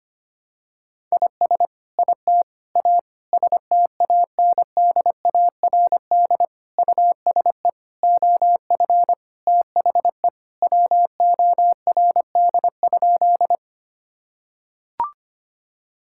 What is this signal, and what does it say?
Morse code, heard as a keyed tone, about 25 wpm, 700 Hz. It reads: IS IT A STANDARD USE OF THE WORD?
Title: Sentences from Top 1000 Words - Review Part 6 - 25wpm